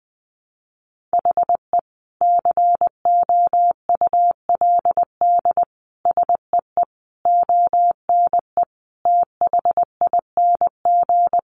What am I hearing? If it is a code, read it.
Morse code, heard as a keyed tone, about 20 wpm, 700 Hz. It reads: HE COULD SEE ONE THING
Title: Sentences from Top 1100 Words 20wpm